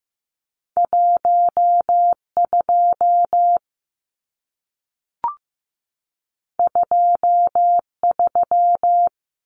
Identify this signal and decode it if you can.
Morse code, heard as a keyed tone, about 15 wpm, 700 Hz. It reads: 12 23
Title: Numbers - 2-Digit 15wpm